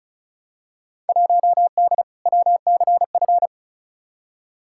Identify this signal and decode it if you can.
Morse code, heard as a keyed tone, about 36 wpm, 700 Hz. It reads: 1D WCF
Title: Field Day Exchanges - 35wpm